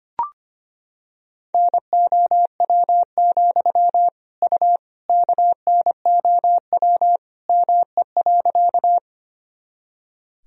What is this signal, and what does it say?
Morse code, heard as a keyed tone, about 25 wpm, 700 Hz. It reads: NOW, U KNOW ME.
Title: Sentences from Top 200 Words and Abbreviations 25wpm